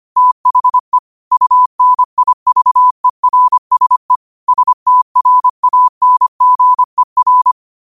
Morse code, heard as a keyed tone, about 25 wpm, 1000 Hz. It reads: THE UNIVERSE STRANGER
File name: qotdfiles / 25wpm